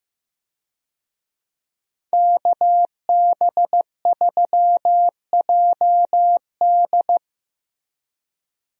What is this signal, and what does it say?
Morse code, heard as a keyed tone, about 15 wpm, 700 Hz. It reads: KB3JD
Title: USA Call Signs - 5 Characters - Repeated 3x - 15wpm